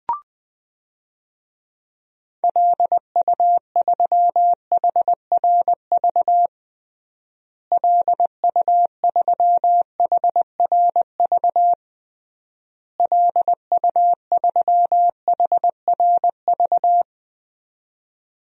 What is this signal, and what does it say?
Morse code, heard as a keyed tone, about 20 wpm, 700 Hz. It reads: LU3HRV LU3HRV LU3HRV
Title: DXCC-100 Call Signs - 6 Characters - Repeated 3x - 20wpm